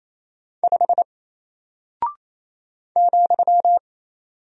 Morse code, heard as a keyed tone, about 27 wpm, 700 Hz. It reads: 5 ,
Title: Single Character (Letter:Number:Punctuation:Prosign) Alt 28wpm